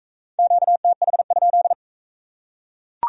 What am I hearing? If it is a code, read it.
Morse code, heard as a keyed tone, about 41 wpm, 700 Hz. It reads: QTH?